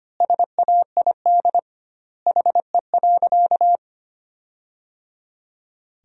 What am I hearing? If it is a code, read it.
Morse code, heard as a keyed tone, about 25 wpm, 700 Hz. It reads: SAID HE.
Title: 2x Word-Spacing - Sentences from Top 300 Words - 25wpm